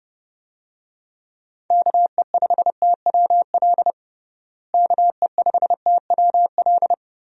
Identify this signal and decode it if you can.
Morse code, heard as a keyed tone, about 29 wpm, 700 Hz. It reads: KE5TWL KE5TWL
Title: USA Call Signs - 6 Characters - Repeated 3x - 30wpm